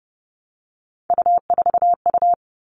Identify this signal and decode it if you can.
Morse code, heard as a keyed tone, about 30 wpm, 700 Hz. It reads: U4U
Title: All Cut Numbers - 3-Digit 30wpm